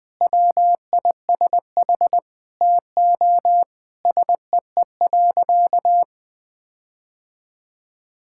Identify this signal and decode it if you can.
Morse code, heard as a keyed tone, about 20 wpm, 700 Hz. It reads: WISH TO SEE.
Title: Sentences with 8-Word Phrases - Spoken Beginning - 20wpm